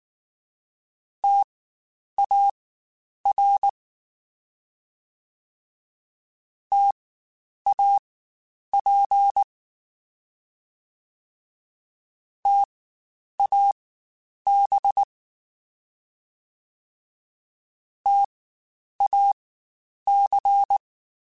Morse code, heard as a keyed tone, about 19 wpm, spaced as for 9 wpm, 800 Hz. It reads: TAR TAP TAB TAC
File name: Spaced Alphas / Spaced Alphas mp3s / Ts sp @18